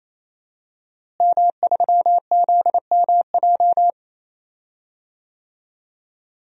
Morse code, heard as a keyed tone, about 28 wpm, 700 Hz. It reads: M3ZMJ
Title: DXCC-100 Call Signs - 5 Characters - Repeated 3x - 28wpm